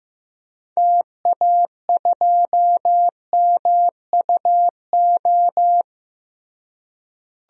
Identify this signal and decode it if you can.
Morse code, heard as a keyed tone, about 15 wpm, 700 Hz. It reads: TA2MUO
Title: DXCC-100 Call Signs - 6 Characters - Repeated 3x - 15wpm